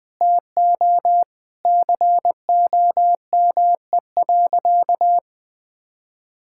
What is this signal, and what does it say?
Morse code, heard as a keyed tone, about 20 wpm, 700 Hz. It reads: TO COME.
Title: Sentences with 2-Word Phrases - 20wpm